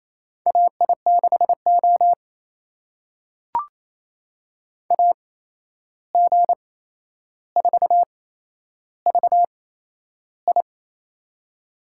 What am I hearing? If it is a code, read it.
Morse code, heard as a keyed tone, about 28 wpm, spaced as for 11 wpm, 700 Hz. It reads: AI6O AG4VI
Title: USA Call Signs - 8x Character Spacing - 28wpm